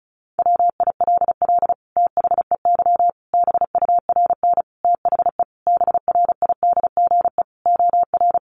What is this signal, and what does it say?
Morse code, heard as a keyed tone, about 34 wpm, 700 Hz. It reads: WILL THEY BURN THE BRIDGE OR
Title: Sentences from Top 1300 Words 35wpm